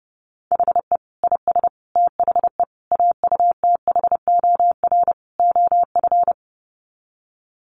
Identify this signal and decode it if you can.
Morse code, heard as a keyed tone, about 30 wpm, 700 Hz. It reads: HE IS THE AUTHOR OF
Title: Sets of 5 Words Encore2 30wpm